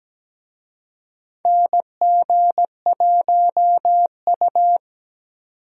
Morse code, heard as a keyed tone, about 17 wpm, 700 Hz. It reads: NG1U
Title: USA Call Signs - 4 Characters - 17wpm